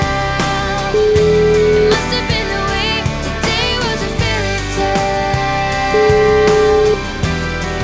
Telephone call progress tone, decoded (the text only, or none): ringback tone